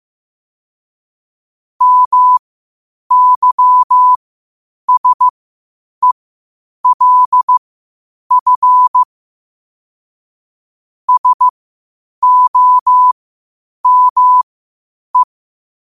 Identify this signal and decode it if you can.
Morse code, heard as a keyed tone, about 15 wpm, spaced as for 8 wpm, 1000 Hz. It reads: MYSELF SOME
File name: qotdfiles / 8wpm